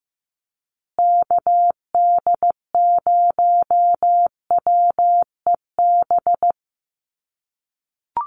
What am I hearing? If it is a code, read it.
Morse code, heard as a keyed tone, about 15 wpm, 700 Hz. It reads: KD0WEB
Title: USA Call Signs Repeated 2x 15wpm